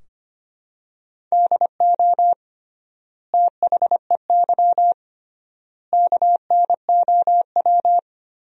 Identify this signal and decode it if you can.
Morse code, heard as a keyed tone, about 25 wpm, 700 Hz. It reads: DO THEY KNOW
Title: 3x Word-Spacing - Sentences from Top 100 Words - 25wpm